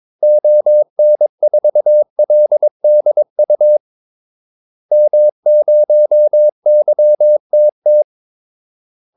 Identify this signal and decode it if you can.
Morse code, heard as a keyed tone, about 22 wpm, 600 Hz. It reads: ON4LDU M0YTT
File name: Callsigns-22wpm